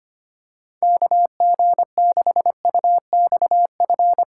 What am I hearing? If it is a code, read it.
Morse code, heard as a keyed tone, about 25 wpm, 700 Hz. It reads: KG6UXF
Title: USA Callsign DE Callsign 25wpm